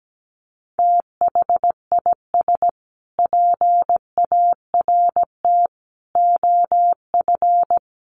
Morse code, heard as a keyed tone, about 17 wpm, 700 Hz. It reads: THIS PART OF
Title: Sets of 5 Words Encore2 17wpm